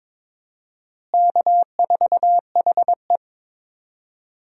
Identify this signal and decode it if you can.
Morse code, heard as a keyed tone, about 22 wpm, 700 Hz. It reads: K4HE
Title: USA Call Signs - 4 Characters - Repeated 2x - 22wpm